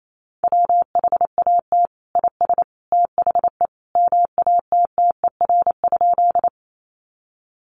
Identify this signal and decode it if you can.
Morse code, heard as a keyed tone, about 28 wpm, 700 Hz. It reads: WHAT IS THE MATTER?
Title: Sentences from Top 700 Words 28wpm